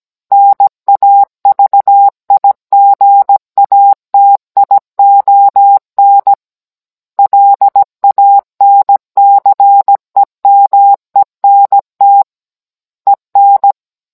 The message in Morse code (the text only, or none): NAVIGATION LANCEMENT EN